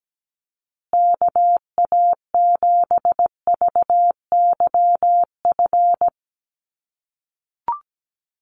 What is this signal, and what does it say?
Morse code, heard as a keyed tone, about 17 wpm, 700 Hz. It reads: KA7VYF
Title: USA Call Signs Repeated 3x 17wpm